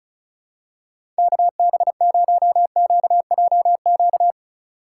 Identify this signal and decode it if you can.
Morse code, heard as a keyed tone, about 34 wpm, 700 Hz. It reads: KB0QJQ
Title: USA Call Signs - 6 Characters - 3x Character Spacing - 35wpm